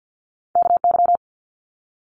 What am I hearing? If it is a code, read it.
Morse code, heard as a keyed tone, about 50 wpm, 700 Hz. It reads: KY
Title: Multi-letter Phonemes Quick Fire 50wpm